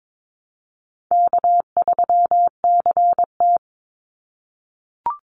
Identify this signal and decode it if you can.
Morse code, heard as a keyed tone, about 22 wpm, 700 Hz. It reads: K3CT
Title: Easy Call Signs - 22wpm